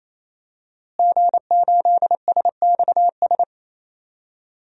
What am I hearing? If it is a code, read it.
Morse code, heard as a keyed tone, about 27 wpm, 700 Hz. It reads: G8SXS
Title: DXCC-100 Call Signs - 5 Characters - Repeated 3x - 28wpm